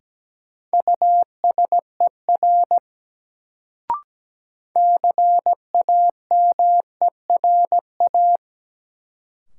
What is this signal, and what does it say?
Morse code, heard as a keyed tone, about 17 wpm, 700 Hz. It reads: USER CAMERA
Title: Sentences from Top 1000 Words - Review Part 5 - New Word Drill - 17wpm